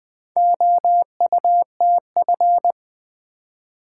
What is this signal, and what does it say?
Morse code, heard as a keyed tone, about 20 wpm, 700 Hz. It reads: OUTF
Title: Build A Word - 2 Letters At A Time - 6-Letter Words - 20wpm